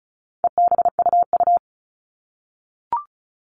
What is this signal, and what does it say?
Morse code, heard as a keyed tone, about 34 wpm, 700 Hz. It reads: EBUU
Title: All Cut Numbers - 4-Digit 35wpm